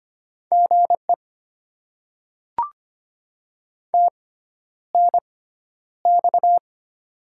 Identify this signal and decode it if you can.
Morse code, heard as a keyed tone, about 25 wpm, spaced as for 11 wpm, 700 Hz. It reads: GE TNX
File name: Top 20 QSO Elements - 6x Character Spacing - 25wpm